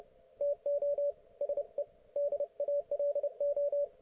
Morse code, heard as a keyed tone, about 30 wpm, 570 Hz. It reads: TO SE DALO